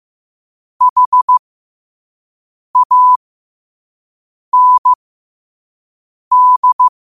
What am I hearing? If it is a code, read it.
Morse code, heard as a keyed tone, about 15 wpm, spaced as for 5 wpm, 1000 Hz. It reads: HAND
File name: qotdfiles / 5wpm